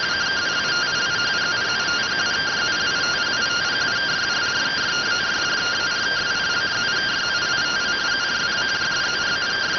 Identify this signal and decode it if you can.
Two-tone teleprinter signal, about 170 Hz shift.